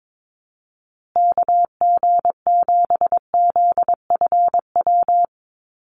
Morse code, heard as a keyed tone, about 22 wpm, 700 Hz. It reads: KG7ZFW
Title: USA Call Signs Repeated 2x 22wpm